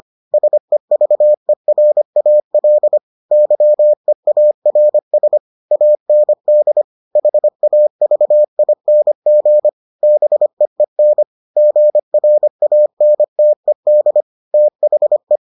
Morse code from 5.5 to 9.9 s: AND HAVING

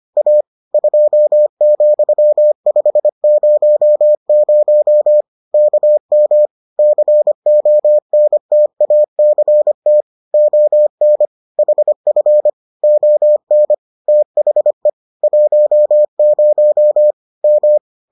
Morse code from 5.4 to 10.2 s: KM CONTACT

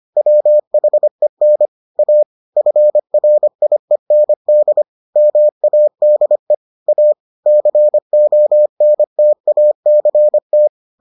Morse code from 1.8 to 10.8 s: A FRIEND MADE A CONTACT

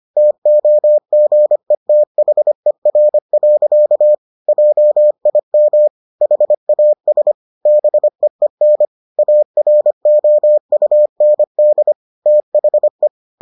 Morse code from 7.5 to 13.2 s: BEEN AROUND THE

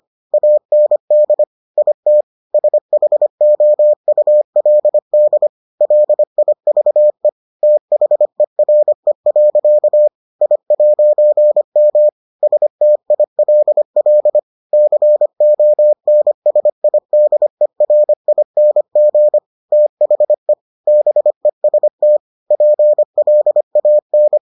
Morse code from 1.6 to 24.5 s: IT SHOULD LIVE THERE. I'M STILL CONSIDERING THE BEST PLAN

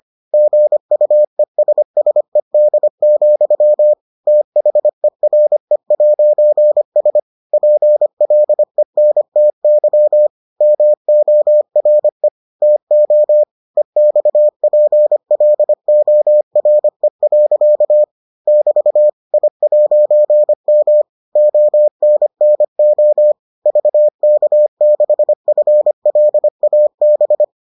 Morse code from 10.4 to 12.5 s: MORE

